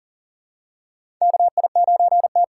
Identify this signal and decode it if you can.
Morse code, heard as a keyed tone, about 40 wpm, 700 Hz. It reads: KI9T
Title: USA Call Signs - 2x Character Spacing - 40wpm